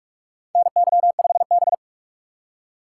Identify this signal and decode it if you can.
Morse code, heard as a keyed tone, about 46 wpm, 700 Hz. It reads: NY5B
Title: USA Call Signs - 3x Character Spacing - 45wpm